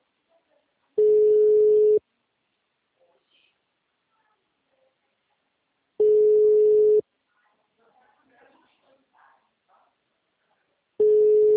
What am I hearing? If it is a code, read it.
Ringback tone (Europe, 425 Hz), with speech.